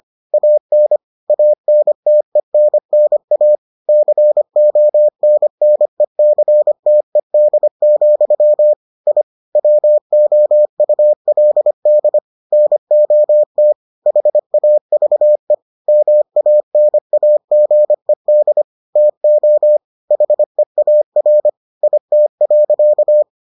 Morse code, heard as a keyed tone, about 25 wpm, 600 Hz. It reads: AN ANTENNA CONNECTED, I WOULD NOT HAVE MANAGED TO HEAR IT.